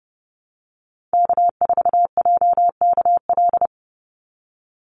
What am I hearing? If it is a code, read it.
Morse code, heard as a keyed tone, about 30 wpm, 700 Hz. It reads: K4JKL
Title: Call Signs Repeated 3x - 30wpm